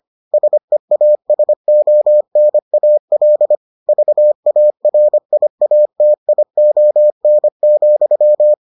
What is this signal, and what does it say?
Morse code, heard as a keyed tone, about 25 wpm, 600 Hz. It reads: SEASONAL VARIATION,